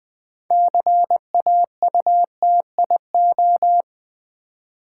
Morse code, heard as a keyed tone, about 20 wpm, 700 Hz. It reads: CAUTIO
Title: Build A Word - 2 Letters At A Time - 8-Letter Words - 20wpm